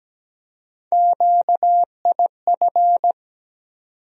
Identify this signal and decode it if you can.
Morse code, heard as a keyed tone, about 17 wpm, 700 Hz. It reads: QIF
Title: USA Call Sign Postfixes - 3 Letters 17wpm